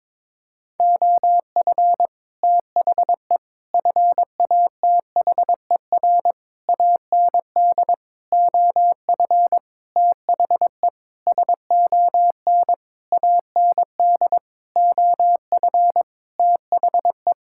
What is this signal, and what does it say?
Morse code, heard as a keyed tone, about 22 wpm, 700 Hz. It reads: OF THE FATHER AND OF THE SON AND OF THE
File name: Sentences with 9-Word Phrases - 22wpm